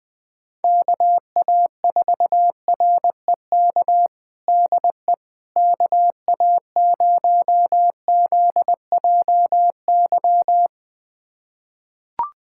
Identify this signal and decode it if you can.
Morse code, heard as a keyed tone, about 21 wpm, 700 Hz. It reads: KA4REK DE KA0ZJY
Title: USA Callsign DE Callsign 20wpm